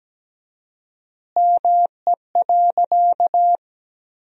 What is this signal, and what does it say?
Morse code, heard as a keyed tone, about 17 wpm, 700 Hz. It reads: ME.
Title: 4x Word-Spacing - Sentences from Top 200 Words - 17wpm